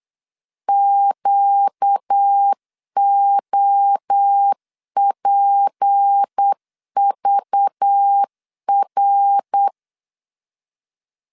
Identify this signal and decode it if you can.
Morse code, heard as a keyed tone, about 8 wpm, 800 Hz. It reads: QOPVR
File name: LEZIONI / 6 PKX0 / 6 PK 8wpm 3003